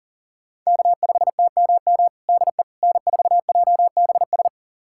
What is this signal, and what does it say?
Morse code, heard as a keyed tone, about 40 wpm, 700 Hz. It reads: K5TMM DE N4JBS
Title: USA Callsign DE Callsign - 5 Characters - 40wpm